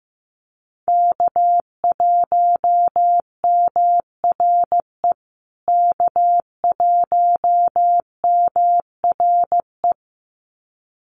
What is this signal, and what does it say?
Morse code, heard as a keyed tone, about 15 wpm, 700 Hz. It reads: K1MRE K1MRE
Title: USA Call Signs Repeated 2x 15wpm